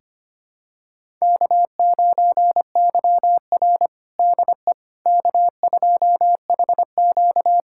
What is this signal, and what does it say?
Morse code, heard as a keyed tone, about 25 wpm, 700 Hz. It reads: K9YR DE K2HQ
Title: USA Callsign DE Callsign - 4 Characters - 25wpm